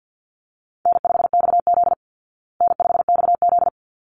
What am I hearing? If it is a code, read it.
Morse code, heard as a keyed tone, about 50 wpm, 700 Hz. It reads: N5XZ N5XZ
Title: Call Signs Repeated 3x - 50wpm